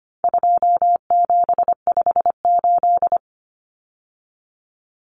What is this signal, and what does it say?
Morse code, heard as a keyed tone, about 25 wpm, 700 Hz. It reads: 2758
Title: Numbers - 4-Digit 25wpm